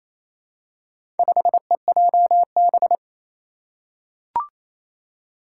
Morse code, heard as a keyed tone, about 28 wpm, 700 Hz. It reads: 5EJB
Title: Four Challenging Letter-Number Combinations 28wpm